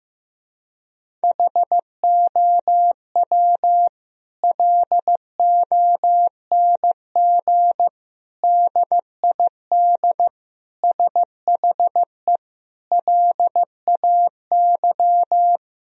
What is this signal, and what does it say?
Morse code, heard as a keyed tone, about 15 wpm, 700 Hz. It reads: HOW LONG DID SHE LAY
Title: Sentences from Top 1000 Words - Review Part 2 - 15wpm